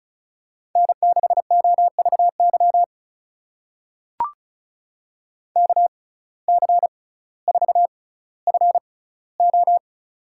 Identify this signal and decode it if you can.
Morse code, heard as a keyed tone, about 36 wpm, spaced as for 18 wpm, 700 Hz. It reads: N6OVY KC4FO
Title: USA Call Signs - 6x Character Spacing - 35wpm